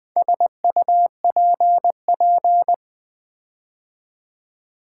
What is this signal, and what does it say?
Morse code, heard as a keyed tone, about 20 wpm, 700 Hz. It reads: SUPP
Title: Build A Word - 4 Letters At A Time - 10-Letter Words - 20wpm